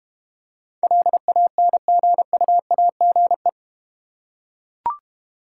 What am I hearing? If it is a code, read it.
Morse code, heard as a keyed tone, about 30 wpm, 700 Hz. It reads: LANGUAGE
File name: Easy Words - Speed-Racing with Hints - 25wpm